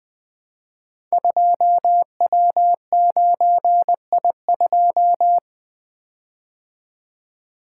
Morse code, heard as a keyed tone, about 20 wpm, 700 Hz. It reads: 2W9I2
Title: Code Groups - 5 Letter:Numbers - 20wpm